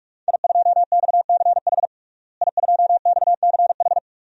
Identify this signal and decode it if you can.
Morse code, heard as a keyed tone, about 41 wpm, 700 Hz. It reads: I2XCH I2XCH
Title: DXCC-100 Call Signs - 5 Characters - Repeated 3x - 45wpm